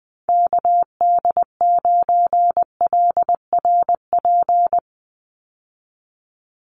Morse code, heard as a keyed tone, about 19 wpm, 700 Hz. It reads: KD9LRP
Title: USA Call Signs 20wpm